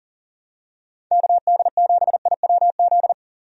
Morse code, heard as a keyed tone, about 40 wpm, 700 Hz. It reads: KD7IWZ
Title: USA Call Signs - 6 Characters - 40wpm